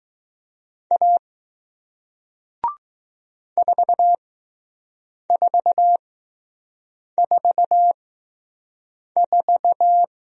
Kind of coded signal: Morse code

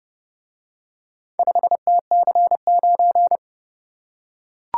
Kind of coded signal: Morse code

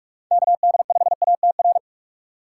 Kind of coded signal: Morse code

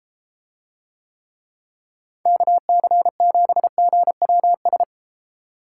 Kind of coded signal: Morse code

